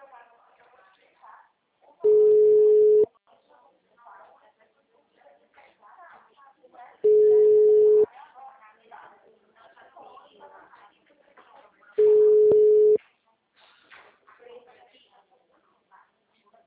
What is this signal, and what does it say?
Ringback tone (Europe, 425 Hz), with speech.
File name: out- -2005- - - .202